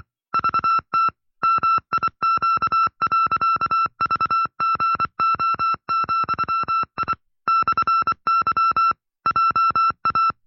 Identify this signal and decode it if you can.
Morse code, heard as a keyed tone, about 24 wpm, 1340 Hz. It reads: VT MIQ.VGO,I /Y JA
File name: FreeFiles / CW-practice / 20WPM-side1